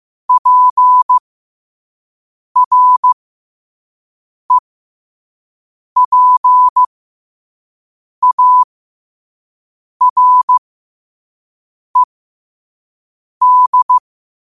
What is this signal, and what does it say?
Morse code, heard as a keyed tone, about 15 wpm, spaced as for 5 wpm, 1000 Hz. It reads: PREPARED
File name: qotdfiles / 5wpm